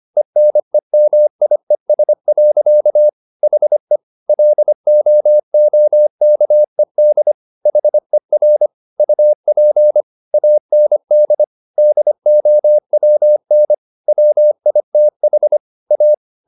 Morse code, heard as a keyed tone, about 25 wpm, 600 Hz. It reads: ENEMIES. HE LOOKED HER UP AND DOWN WITH A